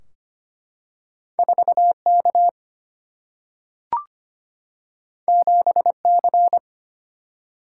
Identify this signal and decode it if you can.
Morse code, heard as a keyed tone, about 25 wpm, 700 Hz. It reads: 4K 7C
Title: Single-Letter Single-Number 25wpm